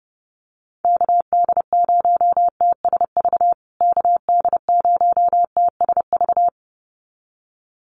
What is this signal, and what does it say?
Morse code, heard as a keyed tone, about 30 wpm, 700 Hz. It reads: KD0TSV KD0TSV
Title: USA Call Signs Repeated 2x 30wpm